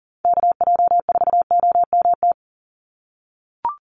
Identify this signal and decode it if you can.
Morse code, heard as a keyed tone, about 40 wpm, 700 Hz. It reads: KJ4OMT
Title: USA Call Signs 40wpm